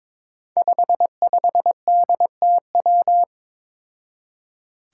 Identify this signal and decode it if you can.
Morse code, heard as a keyed tone, about 22 wpm, 700 Hz. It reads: 55DTW
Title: Five Challenging Letter-Number Combinations 22wpm